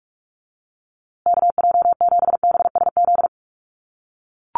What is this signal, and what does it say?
Morse code, heard as a keyed tone, about 44 wpm, 700 Hz. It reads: KJ7BSZ
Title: USA Call Signs Repeated 2x 45wpm